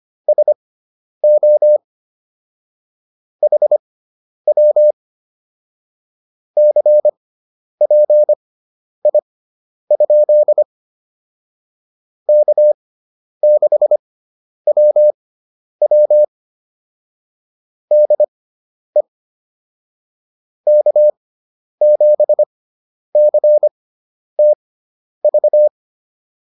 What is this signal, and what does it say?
Morse code, heard as a keyed tone, about 25 wpm, spaced as for 10 wpm, 600 Hz. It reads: SO HW CPI? K6WW DE K7CTV